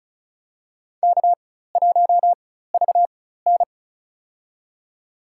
Morse code, heard as a keyed tone, about 36 wpm, spaced as for 16 wpm, 700 Hz. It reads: K1VN
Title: USA Call Signs - 4 Characters - 4x Character Spacing - 35wpm